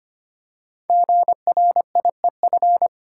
Morse code, heard as a keyed tone, about 25 wpm, 700 Hz. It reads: GRIEF
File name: 5 Letter Words - First 3 Letters Given - 25wpm